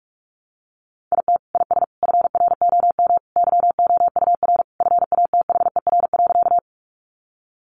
Morse code, heard as a keyed tone, about 44 wpm, 700 Hz. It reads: IT IS FROM YOUR FATHER.
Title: Sentences from Top 300 Words 45wpm